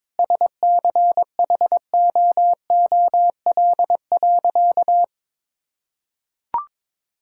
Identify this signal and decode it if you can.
Morse code, heard as a keyed tone, about 22 wpm, 700 Hz. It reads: SCHOOL.
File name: practice / 22wpm / Sentences with 3-Word Phrases - 22wpm